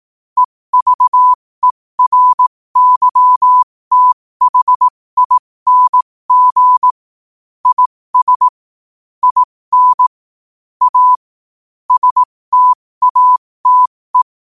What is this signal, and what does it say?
Morse code, heard as a keyed tone, about 18 wpm, spaced as for 15 wpm, 1000 Hz. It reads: EVERYTHING IS IN A STATE
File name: qotdfiles / 15wpm